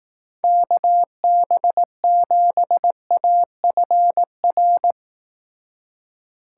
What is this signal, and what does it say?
Morse code, heard as a keyed tone, about 18 wpm, 700 Hz. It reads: KB7AFR
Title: Speed Racing - USA 2x3 Call Signs 15wpm